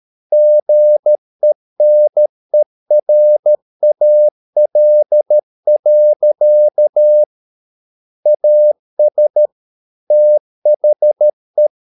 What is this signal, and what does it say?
Morse code, heard as a keyed tone, about 13 wpm, 600 Hz. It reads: GENERAL. AS THE